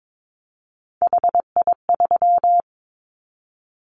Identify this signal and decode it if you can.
Morse code, heard as a keyed tone, about 22 wpm, 700 Hz. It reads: HI3
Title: Callsign Prefixes for Top 100 DXCC - 22wpm